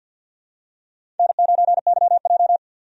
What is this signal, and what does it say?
Morse code, heard as a keyed tone, about 53 wpm, 700 Hz. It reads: N9YJ
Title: USA Call Signs - 4 Characters - 50wpm